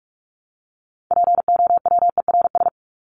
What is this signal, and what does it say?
Morse code, heard as a keyed tone, about 44 wpm, 700 Hz. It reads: POWERS
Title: ES:S Suffix - Top 100 Words with Base Words 45wpm